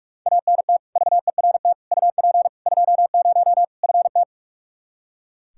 Morse code, heard as a keyed tone, about 44 wpm, 700 Hz. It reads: ANT VERT UP 20 FT
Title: K1USN SST QSO Practice 45wpm